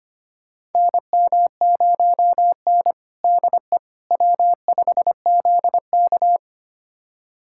Morse code, heard as a keyed tone, about 25 wpm, 700 Hz. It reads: NM0N DE W5ZK
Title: USA Callsign DE Callsign - 4 Characters - 25wpm